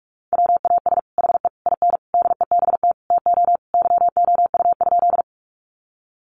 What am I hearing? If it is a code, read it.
Morse code, heard as a keyed tone, about 44 wpm, 700 Hz. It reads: WAS HE IN DEBT TO YOU?
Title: Sentences from Top 1300 Words 45wpm